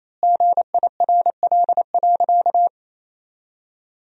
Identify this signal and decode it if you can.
Morse code, heard as a keyed tone, about 27 wpm, 700 Hz. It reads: GIRL.
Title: 4x Word-Spacing - Sentences from Top 400 Words - 28wpm